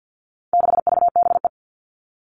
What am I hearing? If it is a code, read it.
Morse code, heard as a keyed tone, about 50 wpm, 700 Hz. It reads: 6VBE
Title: All Cut Numbers - 4-Digit 50wpm